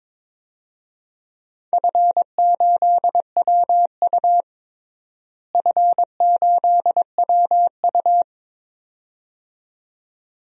Morse code, heard as a keyed tone, about 22 wpm, 700 Hz. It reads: F8WU F8WU
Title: DXCC-100 Call Signs - 4 Characters - Repeated 2x - 22wpm